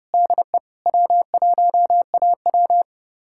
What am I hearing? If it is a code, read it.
Morse code, heard as a keyed tone, about 37 wpm, 700 Hz. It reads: DE W1AW
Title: Speed-Racing Common QSO Phrases - 30wpm